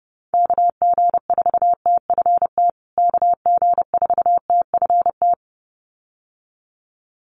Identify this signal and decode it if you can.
Morse code, heard as a keyed tone, about 30 wpm, 700 Hz. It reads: KG4TFT KG4TFT
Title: USA Call Signs Repeated 2x 30wpm